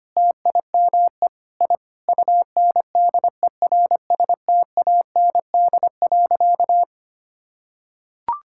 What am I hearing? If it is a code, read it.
Morse code, heard as a keyed tone, about 25 wpm, 700 Hz. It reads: TIME I UNDERSTAND.
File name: practice / 25wpm / Sentences with 4-Word Phrases - 25wpm